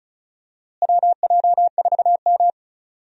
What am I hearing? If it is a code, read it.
Morse code, heard as a keyed tone, about 34 wpm, 700 Hz. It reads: WJ4M
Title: USA Call Signs - 4 Characters - Repeated 3x - 35wpm